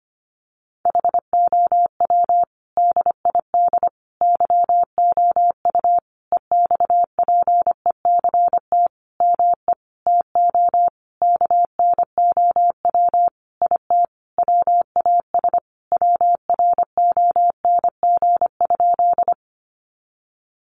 Morse code, heard as a keyed tone, about 25 wpm, 700 Hz. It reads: HOW DID YOU EXPECT ME TO KNOW IT WAS WRONG?
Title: Sentences from Top 1100 Words 25wpm